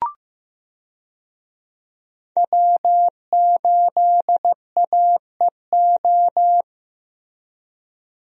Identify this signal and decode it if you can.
Morse code, heard as a keyed tone, about 16 wpm, 700 Hz. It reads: W8AEO
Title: USA Call Signs - 5 Characters - Repeated 2x - 15wpm